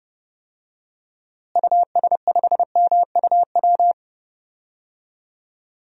Morse code, heard as a keyed tone, about 30 wpm, 700 Hz. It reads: US5MUW
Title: DXCC-100 Call Signs - 6 Characters - Repeated 3x - 30wpm